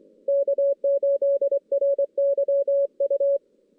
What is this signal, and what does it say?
Morse code, heard as a keyed tone, about 23 wpm, 560 Hz. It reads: K8RYU
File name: 2016 JAN CW NAQP / 2016 Jan naqp k5ot - 0-5